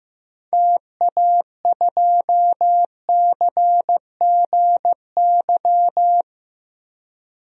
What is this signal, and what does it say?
Morse code, heard as a keyed tone, about 15 wpm, 700 Hz. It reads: TA2CGY
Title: DXCC-100 Call Signs - 6 Characters - Repeated 2x - 15wpm